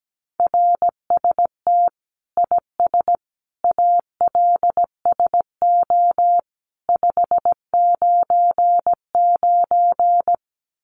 Morse code, heard as a keyed tone, about 17 wpm, 700 Hz. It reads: RST IS ALSO 599
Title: Common QSO Phrases 17wpm